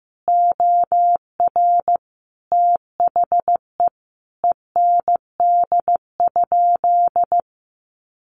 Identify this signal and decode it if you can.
Morse code, heard as a keyed tone, about 14 wpm, 700 Hz. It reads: OR THE END?
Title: Sentences from Top 800 Words 15wpm